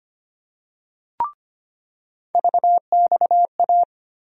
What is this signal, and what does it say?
Morse code, heard as a keyed tone, about 25 wpm, 700 Hz. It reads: VXA
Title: USA Call Sign Postfixes - 3 Letters 25wpm